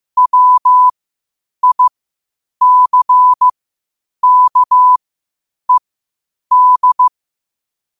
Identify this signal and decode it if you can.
Morse code, heard as a keyed tone, about 15 wpm, spaced as for 8 wpm, 1000 Hz. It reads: WICKED
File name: qotdfiles / 8wpm